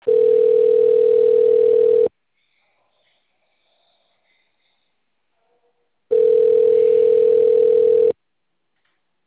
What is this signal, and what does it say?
Ringback tone (North America), 440 + 480 Hz.